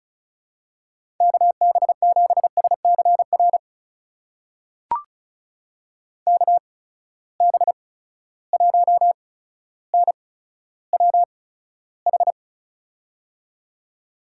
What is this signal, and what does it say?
Morse code, heard as a keyed tone, about 34 wpm, spaced as for 15 wpm, 700 Hz. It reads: KB7SCR KB1NWH
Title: USA Call Signs - 6 Characters - 8x Character Spacing - 35wpm